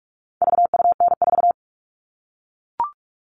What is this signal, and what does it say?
Morse code, heard as a keyed tone, about 44 wpm, 700 Hz. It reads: VUN4
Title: All Cut Numbers - 4-Digit 45wpm